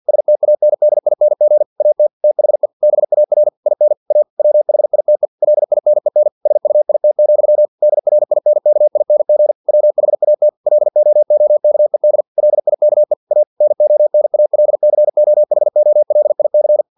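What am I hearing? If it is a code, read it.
Morse code, heard as a keyed tone, about 50 wpm, 600 Hz. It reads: STANDING AT THE BAR IN A WHITE LINEN SUIT, DRINKING WHAT LOOKED LIKE A NONALCOHOLIC
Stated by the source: Morse practice text from an English-language novel at 49 wpm.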